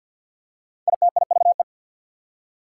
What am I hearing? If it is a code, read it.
Morse code, heard as a keyed tone, about 48 wpm, 690 Hz. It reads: ITIVE
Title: Prefixes and Suffixes 50wpm